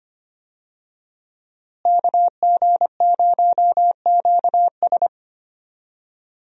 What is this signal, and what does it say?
Morse code, heard as a keyed tone, about 25 wpm, 700 Hz. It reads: KG0QS
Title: USA Call Signs - 5 Characters - Repeated 2x - 25wpm